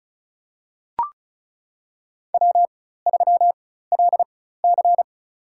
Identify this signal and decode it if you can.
Morse code, heard as a keyed tone, about 34 wpm, spaced as for 16 wpm, 700 Hz. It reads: W3LC
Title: USA Call Signs - 4 Characters - 4x Character Spacing - 35wpm